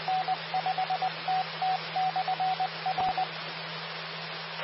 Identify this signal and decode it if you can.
Morse code, heard as a keyed tone, about 20 wpm, 750 Hz, with radio noise. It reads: N5TT/R
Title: Austin, TX Amateur Repeaters